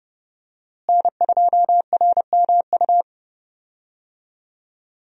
Morse code, heard as a keyed tone, about 30 wpm, 700 Hz. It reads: N2RMU
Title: USA Call Signs - 5 Characters - Repeated 2x - 30wpm